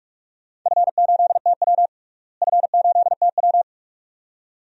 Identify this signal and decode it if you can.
Morse code, heard as a keyed tone, about 44 wpm, 700 Hz. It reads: F8TW F8TW
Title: DXCC-100 Call Signs - 4 Characters - Repeated 2x - 45wpm